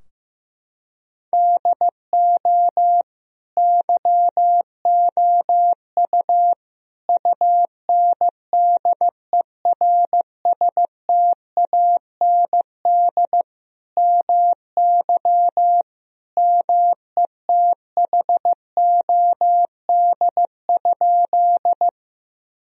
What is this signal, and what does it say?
Morse code, heard as a keyed tone, about 15 wpm, 700 Hz. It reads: DO YOU UNDERSTAND MY METHOD?
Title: Sentences from Top 1000 Words - Review Part 3 - 15wpm